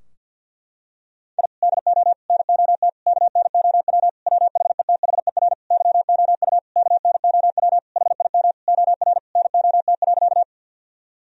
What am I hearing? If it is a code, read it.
Morse code, heard as a keyed tone, about 51 wpm, 700 Hz. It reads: I DO NOT KNOW WHETHER YOU KNOW HIM OR NOT.
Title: Sentences from Top 1000 Words - Review Part 3 - 50wpm